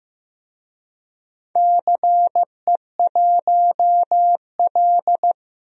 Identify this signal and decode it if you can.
Morse code, heard as a keyed tone, about 15 wpm, 700 Hz. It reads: CE1L
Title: DXCC-100 Call Signs - 4 Characters - Repeated 2x - 15wpm